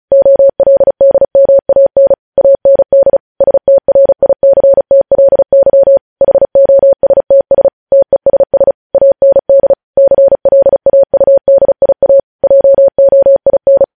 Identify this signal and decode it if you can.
Morse code, heard as a keyed tone, about 36 wpm, 550 Hz. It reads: OLDMAN AND STRICTLY HOSTS TESS AND CLAUDIA JOIN
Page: news headlines in morse code at 35 wpm